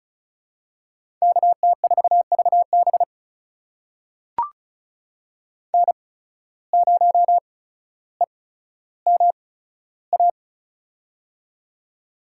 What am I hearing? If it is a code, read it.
Morse code, heard as a keyed tone, about 36 wpm, spaced as for 15 wpm, 700 Hz. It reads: KT4VB N0EMA
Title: USA Call Signs - 5 Characters - 8x Character Spacing - 35wpm